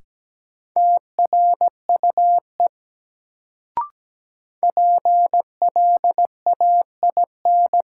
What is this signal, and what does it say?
Morse code, heard as a keyed tone, about 17 wpm, 700 Hz. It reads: TRUE PLAIN
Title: Sentences from Top 400 Words - New Word Drill - 17wpm